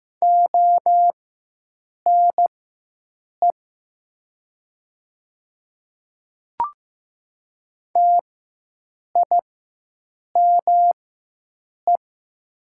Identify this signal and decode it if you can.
Morse code, heard as a keyed tone, about 15 wpm, spaced as for 7 wpm, 700 Hz. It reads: ONE TIME
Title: 4x Character Spacing - Top 100 Words - 15wpm